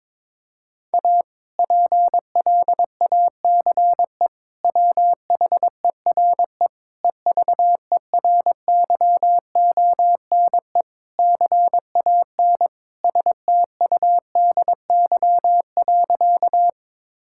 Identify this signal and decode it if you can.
Morse code, heard as a keyed tone, about 22 wpm, 700 Hz. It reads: A PLACE WHERE EVERYONE CAN STUDY.
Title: Sentences from Top 1000 Words - Review Part 5 - 22wpm